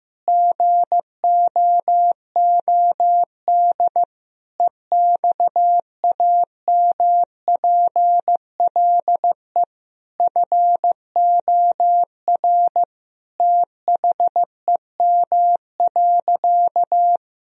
Morse code, heard as a keyed tone, about 15 wpm, 700 Hz. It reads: GOOD EXAMPLE FOR THEM.
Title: Sentences from Top 1000 Words - Review Part 1 - 15wpm